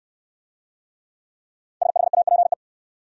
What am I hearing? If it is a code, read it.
Morse code, heard as a keyed tone, about 64 wpm, 700 Hz. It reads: SHAPE